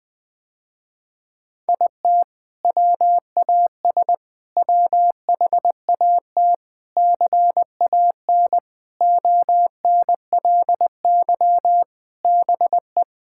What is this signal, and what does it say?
Morse code, heard as a keyed tone, about 20 wpm, 700 Hz. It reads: IT WAS WHAT CAN ONLY BE
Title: Sentences with 6-Word Phrases - Spoken Beginning - 20wpm